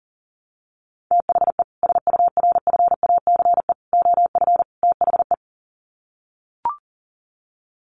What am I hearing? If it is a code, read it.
Morse code, heard as a keyed tone, about 40 wpm, 700 Hz. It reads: THE SURFACE OF THE
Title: Sets of 4 Words 40wpm